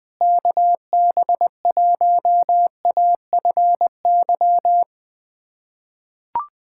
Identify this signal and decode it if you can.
Morse code, heard as a keyed tone, about 20 wpm, 700 Hz. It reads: KB1AFY
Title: USA Callsign DE Callsign - 6 Characters - 20wpm